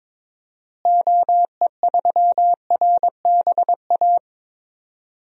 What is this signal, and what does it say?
Morse code, heard as a keyed tone, about 22 wpm, 700 Hz. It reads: OE3RBA
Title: DXCC-100 Call Signs - 6 Characters - Repeated 3x - 22wpm